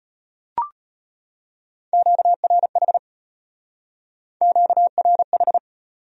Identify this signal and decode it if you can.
Morse code, heard as a keyed tone, about 34 wpm, 700 Hz. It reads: QRH QRH